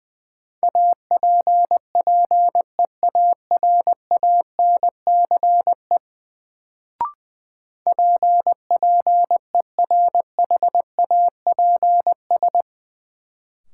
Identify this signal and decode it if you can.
Morse code, heard as a keyed tone, about 20 wpm, 700 Hz. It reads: APPEARANCE PPERHAPS
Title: Sentences from Top 1100 Words - New Word Drill Alt - 20wpm